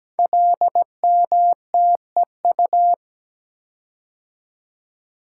Morse code, heard as a keyed tone, about 17 wpm, 700 Hz. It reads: LMTEU
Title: Code Groups - 5 Letters - 17wpm